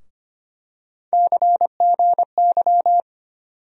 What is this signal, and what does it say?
Morse code, heard as a keyed tone, about 25 wpm, 700 Hz. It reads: CGY
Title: USA Call Sign Postfixes - 3 Letters 25wpm